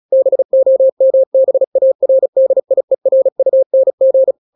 Morse code, heard as a keyed tone, about 36 wpm, 530 Hz. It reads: BOMBARDIERUNG